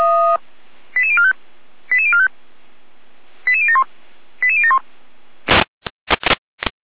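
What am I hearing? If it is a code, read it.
Five-tone selective call (ZVEI), heard as quick runs of five tones, tones of about 70 ms. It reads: 88045 88045 88071 88071
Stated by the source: ZVEI five-tone selective call received by SDR.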